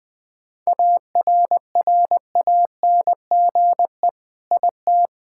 Morse code, heard as a keyed tone, about 21 wpm, 700 Hz. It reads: ARRANGE IT
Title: Sentences with 5-Word Phrases - Spoken Beginning - 20wpm